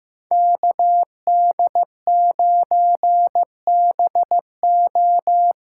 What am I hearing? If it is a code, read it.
Morse code, heard as a keyed tone, about 15 wpm, 700 Hz. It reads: KD9BO
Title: USA Call Signs - 6 Characters - 6x Character Spacing - 15wpm